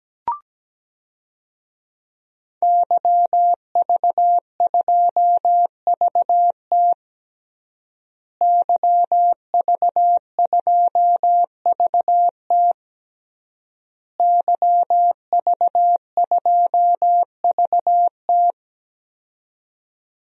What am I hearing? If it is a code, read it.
Morse code, heard as a keyed tone, about 17 wpm, 700 Hz. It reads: YV2VT YV2VT YV2VT
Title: DXCC-100 Call Signs - 5 Characters - Repeated 3x - 17wpm